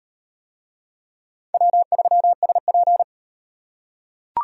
Morse code, heard as a keyed tone, about 34 wpm, 700 Hz. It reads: W3SP